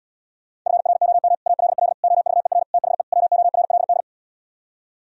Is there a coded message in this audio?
Yes: Morse code from 0.7 to 4.0 s, heard as a keyed tone, about 77 wpm, 700 Hz. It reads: FROM ALL OVER THE WORLD